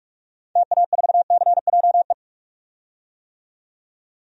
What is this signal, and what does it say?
Morse code, heard as a keyed tone, about 43 wpm, 700 Hz. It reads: TA4CJE
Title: DXCC-100 Call Signs - 6 Characters - 45wpm